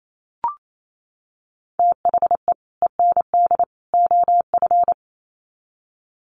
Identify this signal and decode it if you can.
Morse code, heard as a keyed tone, about 28 wpm, 700 Hz. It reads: THE END OF
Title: Sets of 3 Words 28wpm